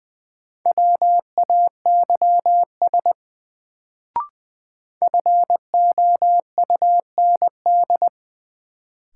Morse code, heard as a keyed tone, about 20 wpm, 700 Hz. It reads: WAYS FOUND
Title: Sets of 5 Words Encore - New Word Drill - 20wpm